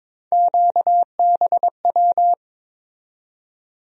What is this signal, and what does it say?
Morse code, heard as a keyed tone, about 22 wpm, 700 Hz. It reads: QBW
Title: USA Call Sign Postfixes - 3 Letters 22wpm